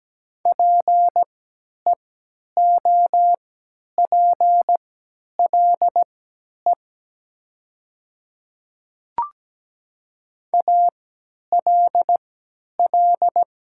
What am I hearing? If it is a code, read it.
Morse code, heard as a keyed tone, about 17 wpm, spaced as for 10 wpm, 700 Hz. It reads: PEOPLE ALL
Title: 3x Character Spacing - Top 100 Words - 17wpm